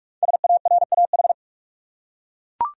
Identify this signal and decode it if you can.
Morse code, heard as a keyed tone, about 41 wpm, 700 Hz. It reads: SARAH
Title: USA Common Names - Max 5 Letters - 45wpm